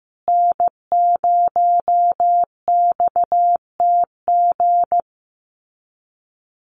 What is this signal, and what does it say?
Morse code, heard as a keyed tone, about 15 wpm, 700 Hz. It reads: N0XTG
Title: USA Call Signs Repeated 3x 15wpm